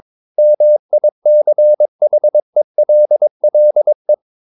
Morse code, heard as a keyed tone, about 22 wpm, 600 Hz. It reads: MICHELLE
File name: Ferie QRS 22wpm